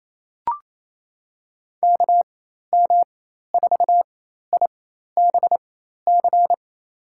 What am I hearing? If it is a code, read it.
Morse code, heard as a keyed tone, about 28 wpm, spaced as for 13 wpm, 700 Hz. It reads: KM4IBC
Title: USA Call Signs - 6 Characters - 4x Character Spacing - 28wpm